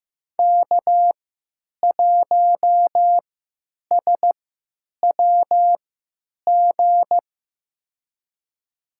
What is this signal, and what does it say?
Morse code, heard as a keyed tone, about 15 wpm, spaced as for 9 wpm, 700 Hz. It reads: K1SWG
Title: USA Call Signs - 5 Characters - 3x Character Spacing - 15wpm